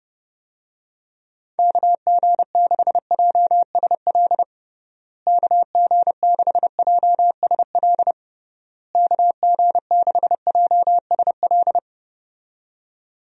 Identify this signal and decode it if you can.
Morse code, heard as a keyed tone, about 30 wpm, 700 Hz. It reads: KG6JSL KG6JSL KG6JSL
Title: USA Call Signs - 6 Characters - Repeated 3x - 30wpm